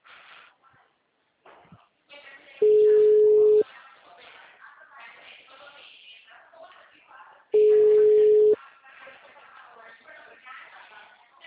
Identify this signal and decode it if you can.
Ringback tone (Europe, 425 Hz), with speech.
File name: out- -2012- - - .127